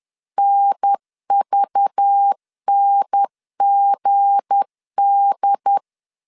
Morse code, heard as a keyed tone, about 10 wpm, 800 Hz. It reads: NVNGD
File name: LEZIONI / 7 GF9Z / 7 G 10wpm 2501